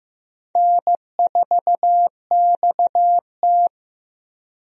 Morse code, heard as a keyed tone, about 15 wpm, 700 Hz. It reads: N4XT
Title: USA Call Signs - 4 Characters - Repeated 2x - 15wpm